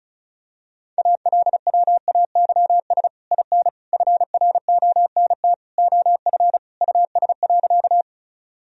Morse code, heard as a keyed tone, about 36 wpm, 700 Hz. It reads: ALWAYS IN FRONT OF US.
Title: Sentences from Top 1000 Words - Review Part 2 - 35wpm